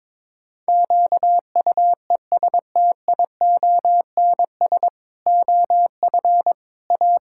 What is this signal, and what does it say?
Morse code, heard as a keyed tone, about 21 wpm, 700 Hz. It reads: QUESTIONS OF A
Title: Pangram Challenge v2 - 22wpm